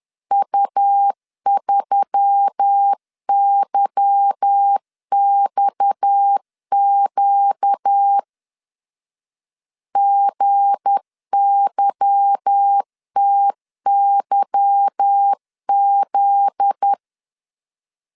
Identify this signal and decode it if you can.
Morse code, heard as a keyed tone, about 10 wpm, 800 Hz. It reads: U3YXQ GYTYZ
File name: LEZIONI / 8 YJ45 / 8 Y 10wpm 2615